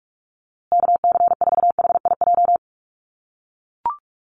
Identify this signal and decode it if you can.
Morse code, heard as a keyed tone, about 44 wpm, 700 Hz. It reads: KC4HIJ
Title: USA Call Signs Repeated 2x 45wpm